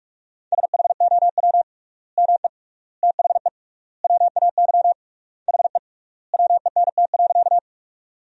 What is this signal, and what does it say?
Morse code, heard as a keyed tone, about 44 wpm, 700 Hz. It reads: SHOW ME THE WAY HE WENT.
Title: 3x Word-Spacing - Sentences from Top 200 Words - 45wpm